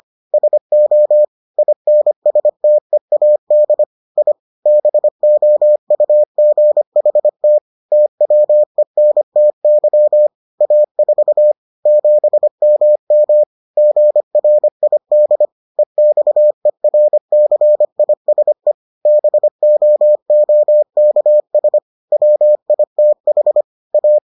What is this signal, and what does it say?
Morse code, heard as a keyed tone, about 25 wpm, 600 Hz. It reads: SO INSTEAD I BOUGHT TWENTY A4 7MM GRID EXERCISE BOOKS WITH A